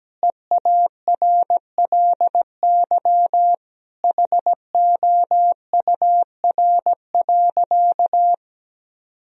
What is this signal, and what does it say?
Morse code, heard as a keyed tone, about 17 wpm, 700 Hz. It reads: EARLY HOUR.
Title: Sentences from Top 1000 Words - Review Part 2 - 17wpm